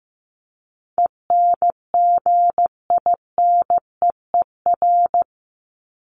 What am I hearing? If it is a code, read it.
Morse code, heard as a keyed tone, about 15 wpm, 700 Hz. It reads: ENGINEER
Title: ING Suffix - Top 100 Words with Base Words 15wpm